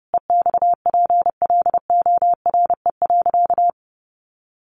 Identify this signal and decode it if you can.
Morse code, heard as a keyed tone, about 30 wpm, 700 Hz. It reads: EXPLORE.
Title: Sentences from Top 1100 Words 30wpm